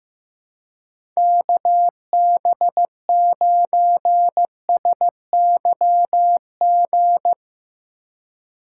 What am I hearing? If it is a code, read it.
Morse code, heard as a keyed tone, about 15 wpm, 700 Hz. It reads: KB9SYG
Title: USA Call Signs - 6 Characters - 3x Character Spacing - 15wpm